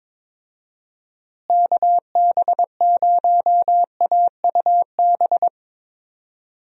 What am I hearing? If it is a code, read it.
Morse code, heard as a keyed tone, about 22 wpm, 700 Hz. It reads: KB0AUB
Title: USA Call Signs - 6 Characters - Repeated 3x - 22wpm